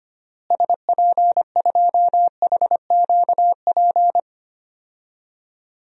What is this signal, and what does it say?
Morse code, heard as a keyed tone, about 25 wpm, 700 Hz. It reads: SP2HQP
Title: DXCC-100 Call Signs - 6 Characters - Repeated 3x - 25wpm